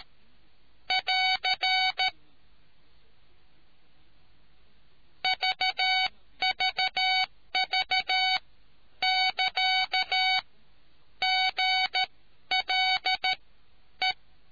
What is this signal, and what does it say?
Morse code, heard as a keyed tone, about 13 wpm, spaced as for 10 wpm, 750 Hz. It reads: <AR> VVV<KA> GLE